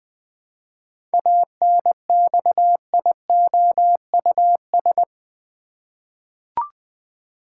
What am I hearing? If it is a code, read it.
Morse code, heard as a keyed tone, about 20 wpm, 700 Hz. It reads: ANXIOUS
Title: Seven Letter Words - 20wpm